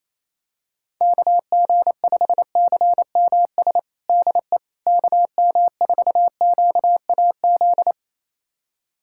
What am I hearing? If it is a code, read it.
Morse code, heard as a keyed tone, about 27 wpm, 700 Hz. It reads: KG5CMS DE KM4QAZ
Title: USA Callsign DE Callsign - 6 Characters - 28wpm